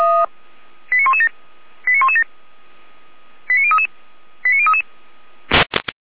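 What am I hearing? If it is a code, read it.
Five-tone selective call (ZVEI), heard as quick runs of five tones, tones of about 70 ms. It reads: 89117 89117 89033 89033
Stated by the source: ZVEI five-tone selective call received by SDR.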